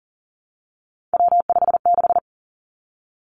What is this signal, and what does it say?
Morse code, heard as a keyed tone, about 38 wpm, 700 Hz. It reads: W56
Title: Three Challenging Letter-Number Combinations 40wpm